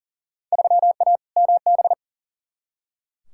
Morse code, heard as a keyed tone, about 40 wpm, 700 Hz. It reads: 3A MB
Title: Field Day Exchanges Encore - 40wpm